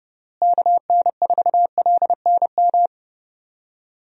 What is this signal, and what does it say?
Morse code, heard as a keyed tone, about 30 wpm, 700 Hz. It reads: KN4LNM
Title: USA Call Signs - 6 Characters - Repeated 2x - 30wpm